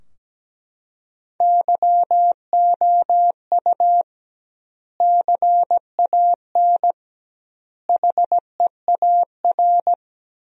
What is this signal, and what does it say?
Morse code, heard as a keyed tone, about 17 wpm, 700 Hz. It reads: YOU CAN HEAR
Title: 2x Word-Spacing - Sentences from Top 500 Words - 17wpm